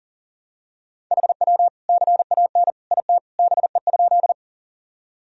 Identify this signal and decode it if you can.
Morse code, heard as a keyed tone, about 40 wpm, 700 Hz. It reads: HW CAN IT BE?
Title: Sentences from Top 100 Words and Abbreviations 40wpm